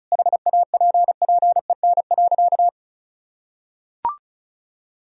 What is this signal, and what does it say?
Morse code, heard as a keyed tone, about 36 wpm, 700 Hz. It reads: HAPPEN.
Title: Sentences from Top 1000 Words - Review Part 4 - 35wpm